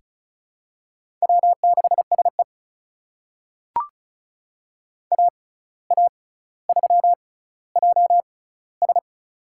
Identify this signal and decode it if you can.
Morse code, heard as a keyed tone, about 36 wpm, spaced as for 17 wpm, 700 Hz. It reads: W6SE AA3JS
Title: USA Call Signs - 6x Character Spacing - 35wpm